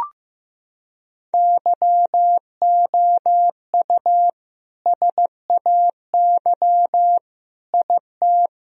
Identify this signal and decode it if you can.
Morse code, heard as a keyed tone, about 15 wpm, 700 Hz. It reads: YOU SAY IT
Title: Sentences from Top 1000 Words - Review Part 1 - 15wpm